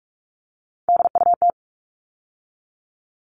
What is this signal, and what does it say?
Morse code, heard as a keyed tone, about 44 wpm, 700 Hz. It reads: DUT
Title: All Cut Numbers - 3-Digit 45wpm